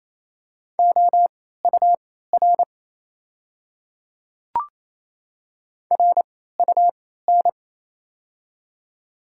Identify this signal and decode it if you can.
Morse code, heard as a keyed tone, about 28 wpm, spaced as for 16 wpm, 700 Hz. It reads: OUR RUN
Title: 3x Character Spacing - Top 300 Words - 28wpm